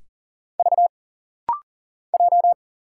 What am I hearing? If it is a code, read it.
Morse code, heard as a keyed tone, about 40 wpm, 700 Hz. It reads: V J
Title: Single Letters - Rapid-Fire 40wpm